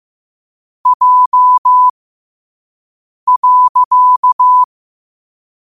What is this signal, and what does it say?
Morse code, heard as a keyed tone, about 15 wpm, spaced as for 5 wpm, 1000 Hz. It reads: J.
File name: qotdfiles / 5wpm